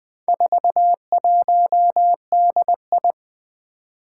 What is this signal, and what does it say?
Morse code, heard as a keyed tone, about 20 wpm, 700 Hz. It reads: 41DI
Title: Four Challenging Letter-Number Combinations 20wpm